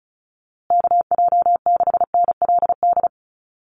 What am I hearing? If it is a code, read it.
Morse code, heard as a keyed tone, about 34 wpm, 700 Hz. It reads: KJ6NLD
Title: USA Call Signs Repeated 2x 35wpm